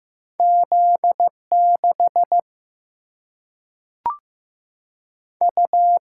Morse code, heard as a keyed tone, about 15 wpm, 700 Hz. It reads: Z 6 U
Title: Single-Letter Single-Number 15wpm